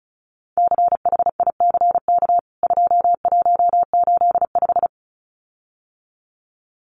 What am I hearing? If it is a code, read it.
Morse code, heard as a keyed tone, about 34 wpm, 700 Hz. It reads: CHICK 2185
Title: CWT Exchange 35wpm